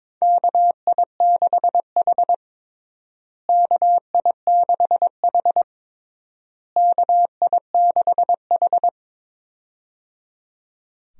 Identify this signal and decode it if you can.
Morse code, heard as a keyed tone, about 22 wpm, 700 Hz. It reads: KI6H KI6H KI6H
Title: USA Call Signs - 4 Characters - Repeated 3x - 22wpm